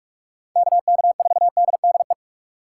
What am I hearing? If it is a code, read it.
Morse code, heard as a keyed tone, about 47 wpm, 700 Hz. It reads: KK4DDE
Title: USA Call Signs - 6 Characters - 6x Character Spacing - 45wpm